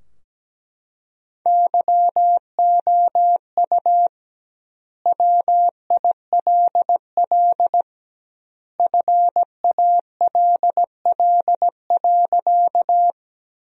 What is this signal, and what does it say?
Morse code, heard as a keyed tone, about 17 wpm, 700 Hz. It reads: YOU WILL FALL.
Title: 2x Word-Spacing - Sentences from Top 500 Words - 17wpm